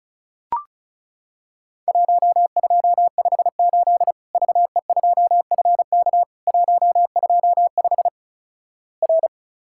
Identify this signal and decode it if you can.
Morse code, heard as a keyed tone, about 36 wpm, 700 Hz. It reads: 1258 VE2FK 125 R
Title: Worked All Europe DX Contest - QTC Exchange - 35wpm